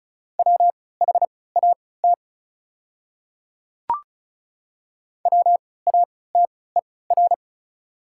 Morse code, heard as a keyed tone, about 36 wpm, spaced as for 20 wpm, 700 Hz. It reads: WHAT WATER
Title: 3x Character Spacing - Top 200 Words - 35wpm